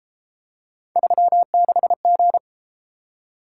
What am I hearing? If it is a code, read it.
Morse code, heard as a keyed tone, about 31 wpm, 700 Hz. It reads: 36G